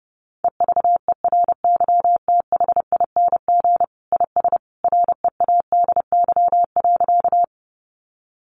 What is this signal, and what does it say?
Morse code, heard as a keyed tone, about 30 wpm, 700 Hz. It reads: EVERYTHING IS READY.
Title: Sentences from Top 700 Words 30wpm